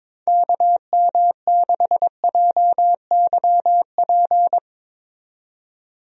Morse code, heard as a keyed tone, about 22 wpm, 700 Hz. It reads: KM6JYP
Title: USA Call Signs - 6 Characters - 22wpm